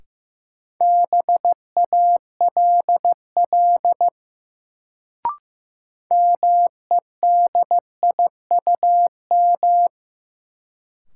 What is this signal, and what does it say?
Morse code, heard as a keyed tone, about 15 wpm, 700 Hz. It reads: BALL MEDIUM
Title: Sentences from Top 900 Words - New Word Drill - 15wpm